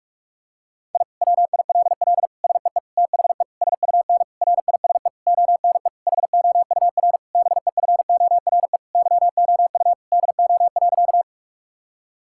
Morse code, heard as a keyed tone, about 41 wpm, 700 Hz. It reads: I WILL SEE THE SUN RISE ONE HOUR BEFORE YOU DO.
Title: Sentences from Top 1000 Words - Review Part 6 - 45wpm